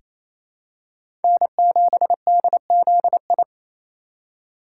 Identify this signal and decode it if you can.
Morse code, heard as a keyed tone, about 28 wpm, 690 Hz. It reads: N7DZI
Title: USA Call Signs - 2x Character Spacing - 28wpm